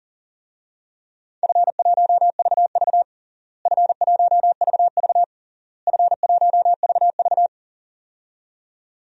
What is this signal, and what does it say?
Morse code, heard as a keyed tone, about 40 wpm, 700 Hz. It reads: F1VV F1VV F1VV
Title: DXCC-100 Call Signs - 4 Characters - Repeated 3x - 40wpm